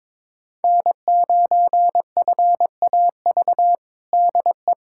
Morse code, heard as a keyed tone, about 23 wpm, 700 Hz. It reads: N9FAV DE
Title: USA Callsign DE Callsign - 5 Characters - 22wpm